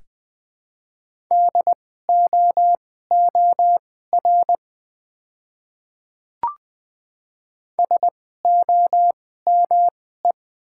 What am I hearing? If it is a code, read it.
Morse code, heard as a keyed tone, about 20 wpm, spaced as for 14 wpm, 700 Hz. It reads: DOOR SOME
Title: 2x Character Spacing - Top 500 Words - 20wpm